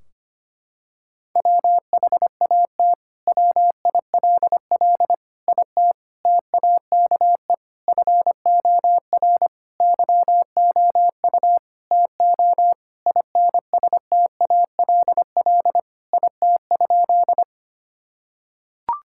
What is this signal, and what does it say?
Morse code, heard as a keyed tone, about 25 wpm, 700 Hz. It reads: WHAT WILL IT TAKE FOR YOU TO INSTALL IT?
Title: Sentences from Top 1600 Words 25wpm